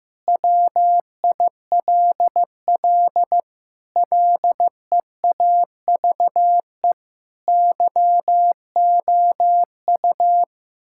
Morse code, heard as a keyed tone, about 15 wpm, 700 Hz. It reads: WILL LEAVE YOU
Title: Sentences from Top 1600 Words 15wpm